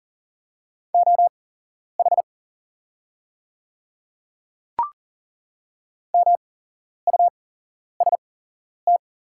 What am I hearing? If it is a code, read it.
Morse code, heard as a keyed tone, about 40 wpm, spaced as for 11 wpm, 700 Hz. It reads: OH MUST
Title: 8x Character Spacing - Top 500 Words - 40wpm